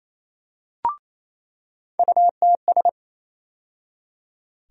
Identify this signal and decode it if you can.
Morse code, heard as a keyed tone, about 28 wpm, 700 Hz. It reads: UTS
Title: USA Call Sign Postfixes - 3 Letters 28wpm